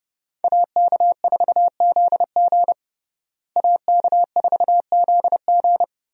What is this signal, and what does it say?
Morse code, heard as a keyed tone, about 30 wpm, 700 Hz. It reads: AK4ZG AK4ZG
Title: USA Call Signs - 5 Characters - Repeated 2x - 30wpm